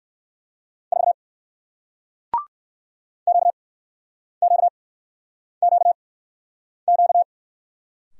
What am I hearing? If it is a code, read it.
Morse code, heard as a keyed tone, about 62 wpm, spaced as for 9 wpm, 700 Hz. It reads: 4 QQQQ